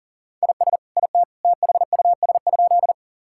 Morse code, heard as a keyed tone, about 40 wpm, 700 Hz. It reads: IS IT THUS?
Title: Sentences from Top 1000 Words - Review Part 5 - 40wpm